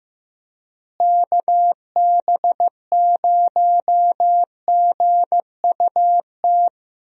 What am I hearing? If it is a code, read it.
Morse code, heard as a keyed tone, about 15 wpm, 700 Hz. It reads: KB0GUT
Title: USA Call Signs - 6 Characters - Repeated 3x - 15wpm